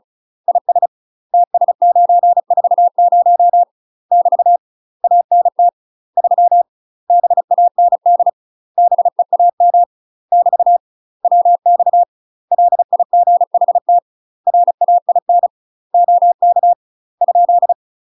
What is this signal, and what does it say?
Morse code, heard as a keyed tone, about 35 wpm, 700 Hz. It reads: IS TS940 <BT> ANT 3 BAND BEAM <BT> WX LIGHT RAIN OK ?
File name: UK MASTER QSOs TRAINING / 35WPM / UK MASTER QSOs 0003